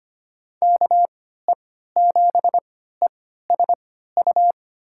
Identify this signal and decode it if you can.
Morse code, heard as a keyed tone, about 25 wpm, spaced as for 14 wpm, 700 Hz. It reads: KE7ESU
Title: USA Call Signs - 6 Characters - 3x Character Spacing - 25wpm